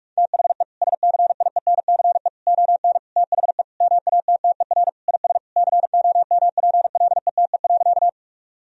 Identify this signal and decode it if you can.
Morse code, heard as a keyed tone, about 45 wpm, 700 Hz. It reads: THE SCIENCE ON THE MATTER IS COMPLETE.
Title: Sentences from Top 1000 Words - Review Part 2 - 45wpm